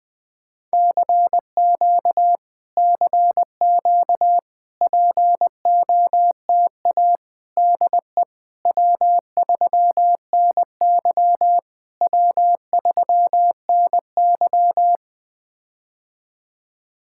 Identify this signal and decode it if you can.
Morse code, heard as a keyed tone, about 20 wpm, 700 Hz. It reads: CQ CQ POTA DE W3NY W3NY
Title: POTA QSO Practice - 20wpm